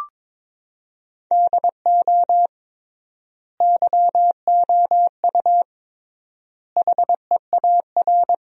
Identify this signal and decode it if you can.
Morse code, heard as a keyed tone, about 22 wpm, 700 Hz. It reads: DO YOU HEAR
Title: 3x Word-Spacing - Sentences from Top 300 Words - 22wpm